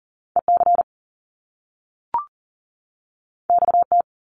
Morse code, heard as a keyed tone, about 41 wpm, 700 Hz. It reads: EC XT
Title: Top Two Letter Combinations in Words 40wpm